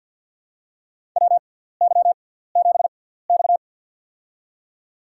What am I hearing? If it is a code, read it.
Morse code, heard as a keyed tone, about 50 wpm, spaced as for 17 wpm, 700 Hz. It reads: WY7X
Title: USA Call Signs - 4 Characters - 6x Character Spacing - 50wpm